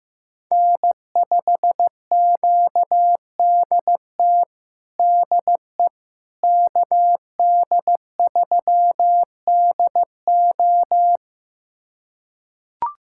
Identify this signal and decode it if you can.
Morse code, heard as a keyed tone, about 16 wpm, 700 Hz. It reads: N5QDT DE KD3DO
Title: USA Callsign DE Callsign - 5 Characters - 15wpm